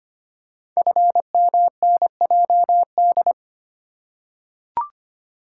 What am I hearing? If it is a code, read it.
Morse code, heard as a keyed tone, about 25 wpm, 700 Hz. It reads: FMNJD
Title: Code Groups - 5 Letters - 25wpm